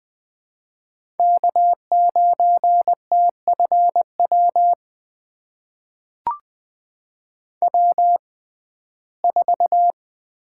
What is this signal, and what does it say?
Morse code, heard as a keyed tone, about 20 wpm, spaced as for 14 wpm, 700 Hz. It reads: K9TFW W4
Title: USA Call Signs - 6x Character Spacing - 20wpm